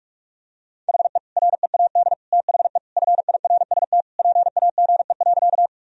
Morse code, heard as a keyed tone, about 41 wpm, 700 Hz. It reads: HE READ THE FIRST PAGE.
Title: Sentences from Top 1000 Words - Review Part 1 - 45wpm